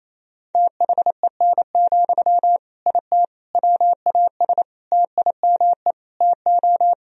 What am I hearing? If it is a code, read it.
Morse code, heard as a keyed tone, about 28 wpm, 700 Hz. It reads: THEN, IT WAS TIME TO
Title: Sentences from Top 1000 Words - Review Part 1 - 28wpm